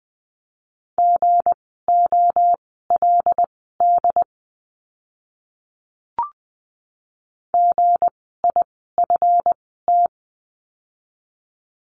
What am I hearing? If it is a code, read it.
Morse code, heard as a keyed tone, about 20 wpm, spaced as for 15 wpm, 700 Hz. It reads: GOLD GIFT
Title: 160 Lesson 26f - Letter g - Words 20wpm